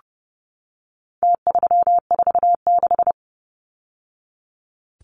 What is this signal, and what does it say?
Morse code, heard as a keyed tone, about 30 wpm, 700 Hz. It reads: T346
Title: Numbers - Contest Style - 4-Digit 30wpm